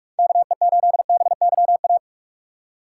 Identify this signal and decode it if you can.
Morse code, heard as a keyed tone, about 46 wpm, 700 Hz. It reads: KE8BYA
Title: USA Call Signs - 8x Character Spacing - 45wpm